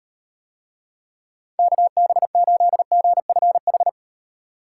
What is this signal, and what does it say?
Morse code, heard as a keyed tone, about 36 wpm, 700 Hz. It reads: KB8GFH